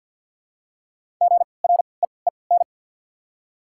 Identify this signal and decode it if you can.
Morse code, heard as a keyed tone, about 52 wpm, spaced as for 29 wpm, 700 Hz. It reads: GREEN
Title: 3x Character Spacing - Top 500 Words - 50wpm